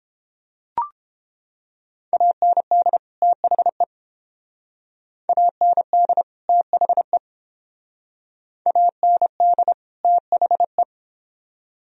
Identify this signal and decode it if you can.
Morse code, heard as a keyed tone, about 28 wpm, 700 Hz. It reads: AND THE AND THE AND THE